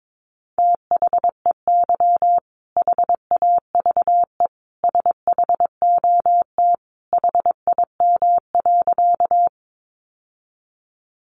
Morse code, heard as a keyed tone, about 22 wpm, 700 Hz. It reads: THEY HAVE SHOT HIM.
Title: Sentences from Top 1100 Words 22wpm